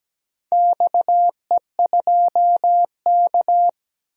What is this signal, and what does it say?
Morse code, heard as a keyed tone, about 17 wpm, 700 Hz. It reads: XE2K
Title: DXCC-100 Call Signs - 4 Characters - 17wpm